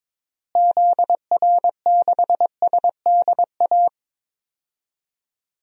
Morse code, heard as a keyed tone, about 22 wpm, 700 Hz. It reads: ZR6SDA
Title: DXCC-100 Call Signs - 6 Characters - Repeated 3x - 22wpm